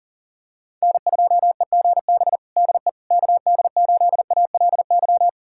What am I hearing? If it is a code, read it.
Morse code, heard as a keyed tone, about 38 wpm, 700 Hz. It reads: N2EGB DE KD8ALY
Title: USA Callsign DE Callsign 40wpm